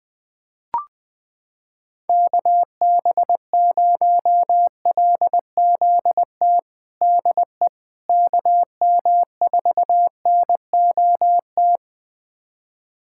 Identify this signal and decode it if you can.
Morse code, heard as a keyed tone, about 20 wpm, 700 Hz. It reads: KB0LZT DE KM4NOT
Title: USA Callsign DE Callsign - 6 Characters - 20wpm